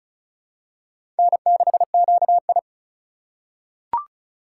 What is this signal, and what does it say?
Morse code, heard as a keyed tone, about 34 wpm, 700 Hz. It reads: N6QI
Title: USA Call Signs - 4 Characters - 4x Character Spacing - 35wpm